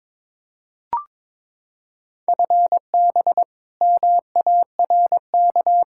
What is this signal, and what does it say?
Morse code, heard as a keyed tone, about 22 wpm, 700 Hz. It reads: FB MARK
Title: K1USN SST QSO Practice 22wpm